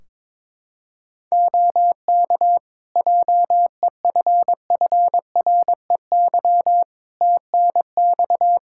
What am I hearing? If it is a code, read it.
Morse code, heard as a keyed tone, about 22 wpm, 700 Hz. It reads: OK JEFFREY TNX
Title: Practice QSO - Exchange 4 - Ending - 22wpm